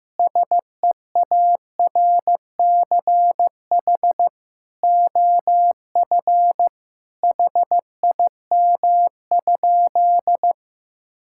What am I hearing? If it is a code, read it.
Morse code, heard as a keyed tone, about 15 wpm, 700 Hz. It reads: SEARCH OF HIM?
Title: Sentences from Top 1000 Words - Review Part 5 - 15wpm